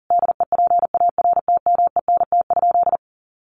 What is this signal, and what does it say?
Morse code, heard as a keyed tone, about 40 wpm, 700 Hz. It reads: DEPARTMENT?
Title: Sentences from Top 1200 Words 40wpm